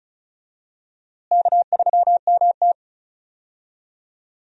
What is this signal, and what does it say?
Morse code, heard as a keyed tone, about 36 wpm, 700 Hz. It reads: K3MT
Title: USA Call Signs - 4 Characters - Repeated 2x - 35wpm